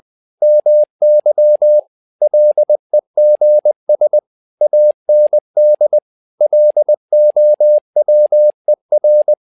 Morse code, heard as a keyed tone, about 20 wpm, 600 Hz. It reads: MY LEGS AND LOWER
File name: tower trouble 20wpm